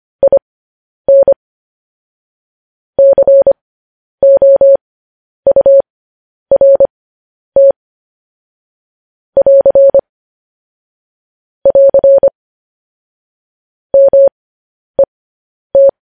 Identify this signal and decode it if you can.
Morse code, heard as a keyed tone, about 25 wpm, spaced as for 10 wpm, 550 Hz. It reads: IN COURT <AR> <AR> MET